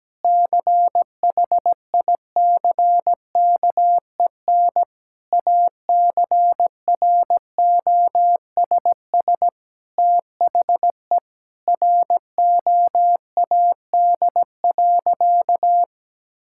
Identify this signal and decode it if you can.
Morse code, heard as a keyed tone, about 17 wpm, 700 Hz. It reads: CHICKEN ACROSS THE ROAD.
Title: Sentences from Top 1000 Words - Review Part 5 - 17wpm